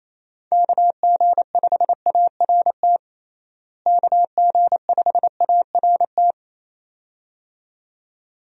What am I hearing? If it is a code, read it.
Morse code, heard as a keyed tone, about 28 wpm, 700 Hz. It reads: KG5ART KG5ART
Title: USA Call Signs - 6 Characters - Repeated 2x - 28wpm